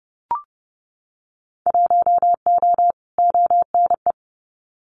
Morse code, heard as a keyed tone, about 30 wpm, 700 Hz. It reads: 1O ONE
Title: Winter Field Day Exchanges 30wpm